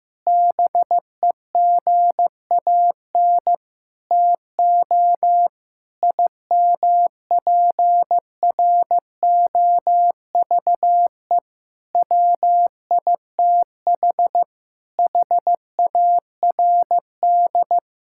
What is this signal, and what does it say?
Morse code, heard as a keyed tone, about 15 wpm, 690 Hz. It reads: BEGAN TO IMPROVE WITH HARD
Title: Sentences from Top 1000 Words - Review Part 5 - 15wpm